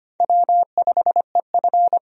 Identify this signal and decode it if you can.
Morse code, heard as a keyed tone, about 25 wpm, 700 Hz. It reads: W5EF
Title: USA Call Signs - 4 Characters - 25wpm